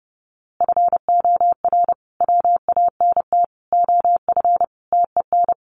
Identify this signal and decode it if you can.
Morse code, heard as a keyed tone, about 30 wpm, 700 Hz. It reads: FOR WANT OF TEN
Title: A Sound Of Thunder 30wpm